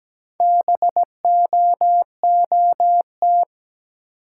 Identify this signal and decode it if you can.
Morse code, heard as a keyed tone, about 17 wpm, 700 Hz. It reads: BOOT
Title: Four Letter Words - 17wpm